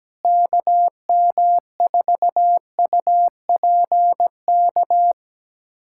Morse code, heard as a keyed tone, about 17 wpm, 700 Hz. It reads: KM4UPK
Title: USA Callsign DE Callsign - 6 Characters - 17wpm